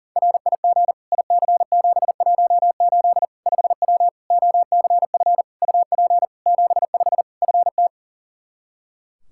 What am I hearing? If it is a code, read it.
Morse code, heard as a keyed tone, about 40 wpm, 700 Hz. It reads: RIG IC718 5W OCF UP 75 FT
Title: K1USN SST QSO Practice 40wpm